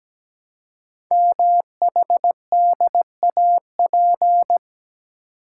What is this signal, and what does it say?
Morse code, heard as a keyed tone, about 17 wpm, 700 Hz. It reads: MHDAP
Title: Code Groups - 5 Letters - 17wpm